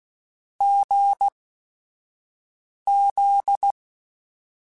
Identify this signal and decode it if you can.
Morse code, heard as a keyed tone, about 16 wpm, spaced as for 5 wpm, 800 Hz. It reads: GZ
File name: Short Runs / Short Runs-mp3s / GZ GZ7 @5